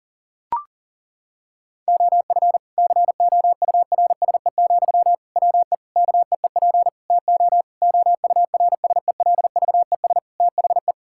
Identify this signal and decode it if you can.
Morse code, heard as a keyed tone, about 40 wpm, 700 Hz. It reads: OF COURSE, WE KEEP TO OURSELVES THE
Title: Sentences from Top 1600 Words 40wpm